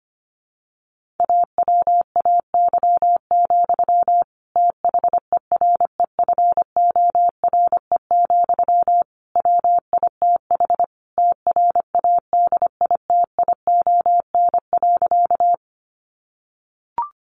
Morse code, heard as a keyed tone, about 25 wpm, 700 Hz. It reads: AWAY, THEREFORE, WITH TRADITION.
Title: Sentences from Top 1500 Words 25wpm